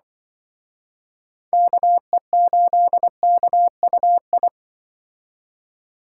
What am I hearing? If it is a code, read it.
Morse code, heard as a keyed tone, about 25 wpm, 700 Hz. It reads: KE8KUI
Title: Speed Racing - USA 2x3 Call Signs 20wpm